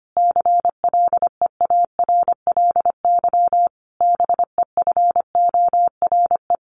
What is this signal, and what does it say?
Morse code, heard as a keyed tone, about 25 wpm, 700 Hz. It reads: CLEARLY BEFORE
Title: Sentences from Top 1300 Words 25wpm